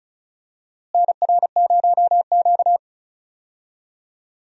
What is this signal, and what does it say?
Morse code, heard as a keyed tone, about 36 wpm, 700 Hz. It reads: NR0Q
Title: USA Call Signs - 4 Characters - Repeated 2x - 35wpm